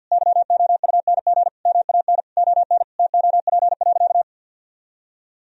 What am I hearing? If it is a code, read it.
Morse code, heard as a keyed tone, about 46 wpm, 700 Hz. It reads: YOUNG MAN ON TOP.
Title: Sentences from Top 1000 Words - Review Part 2 - 50wpm